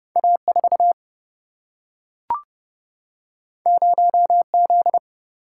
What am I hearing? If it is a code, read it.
Morse code, heard as a keyed tone, about 30 wpm, 700 Hz. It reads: A4 0Z
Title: Single-Letter Single-Number 30wpm